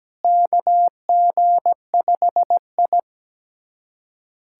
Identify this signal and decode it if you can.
Morse code, heard as a keyed tone, about 17 wpm, 700 Hz. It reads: KG5I
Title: USA Callsign DE Callsign - 4 Characters - 17wpm